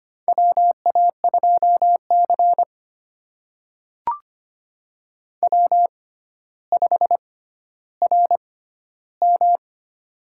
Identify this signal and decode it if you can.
Morse code, heard as a keyed tone, about 25 wpm, spaced as for 13 wpm, 700 Hz. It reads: WA2C W5RM
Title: USA Call Signs - 4 Characters - 6x Character Spacing - 25wpm